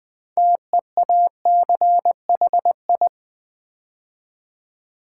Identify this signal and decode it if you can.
Morse code, heard as a keyed tone, about 20 wpm, 700 Hz. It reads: TEACHI
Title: Build A Word - 2 Letters At A Time - 8-Letter Words - 20wpm